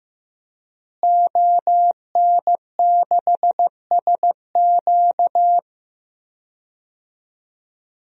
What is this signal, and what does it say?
Morse code, heard as a keyed tone, about 15 wpm, 700 Hz. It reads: ON6SQ
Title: DXCC-100 Call Signs - 5 Characters - Repeated 3x - 15wpm